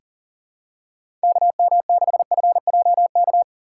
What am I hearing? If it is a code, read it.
Morse code, heard as a keyed tone, about 40 wpm, 690 Hz. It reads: KM6FJK
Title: USA Call Signs - 6x Character Spacing - 40wpm